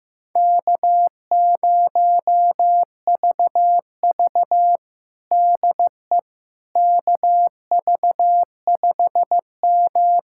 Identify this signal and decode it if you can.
Morse code, heard as a keyed tone, about 15 wpm, 700 Hz. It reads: K0VV DE KV5M
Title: USA Callsign DE Callsign - 4 Characters - 15wpm